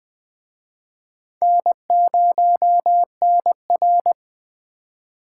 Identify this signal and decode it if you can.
Morse code, heard as a keyed tone, about 21 wpm, 700 Hz. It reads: N0NR